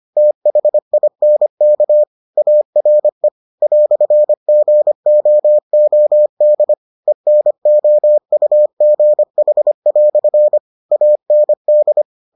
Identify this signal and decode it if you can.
Morse code, heard as a keyed tone, about 25 wpm, 600 Hz. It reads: THINK ARE "GOOD ENOUGH" AND